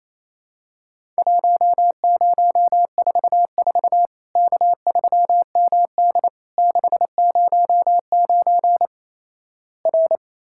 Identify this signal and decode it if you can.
Morse code, heard as a keyed tone, about 28 wpm, 700 Hz. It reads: 1044 K3MD 609 R
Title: Worked All Europe DX Contest - QTC Exchange - 28wpm